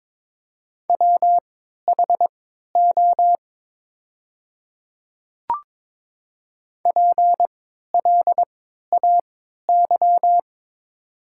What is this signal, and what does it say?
Morse code, heard as a keyed tone, about 22 wpm, spaced as for 13 wpm, 700 Hz. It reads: WHO PLAY
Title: 3x Character Spacing - Top 200 Words - 22wpm